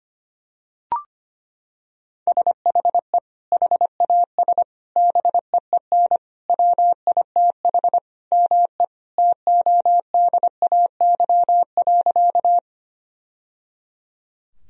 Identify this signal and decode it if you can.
Morse code, heard as a keyed tone, about 25 wpm, 700 Hz. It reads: SHE HAS BEEN WITH ME TODAY.
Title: Sentences from Top 1000 Words - Review Part 4 - 25wpm